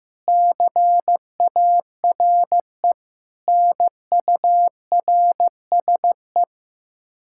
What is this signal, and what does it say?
Morse code, heard as a keyed tone, about 15 wpm, 700 Hz. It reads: CARE NURSE
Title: Practice QSO - Exchange 3 - More Info - 15wpm